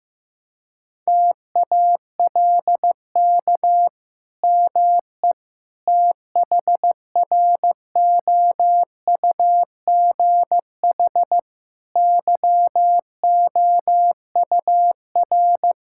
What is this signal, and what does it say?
Morse code, heard as a keyed tone, about 14 wpm, 700 Hz. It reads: TALK ME THROUGH YOUR
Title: Sentences from Top 1000 Words - Review Part 5 - 15wpm